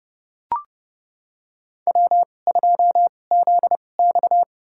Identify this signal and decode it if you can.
Morse code, heard as a keyed tone, about 30 wpm, spaced as for 21 wpm, 700 Hz. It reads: W2ZX
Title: USA Call Signs - 4 Characters - 2x Character Spacing - 30wpm